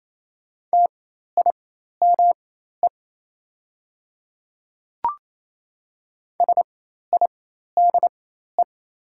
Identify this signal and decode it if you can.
Morse code, heard as a keyed tone, about 28 wpm, spaced as for 13 wpm, 700 Hz. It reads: TIME SIDE
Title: 4x Character Spacing - Top 300 Words - 28wpm